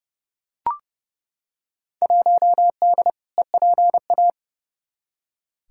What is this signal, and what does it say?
Morse code, heard as a keyed tone, about 32 wpm, 700 Hz. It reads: 1D EPA
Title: Field Day Exchanges - 30wpm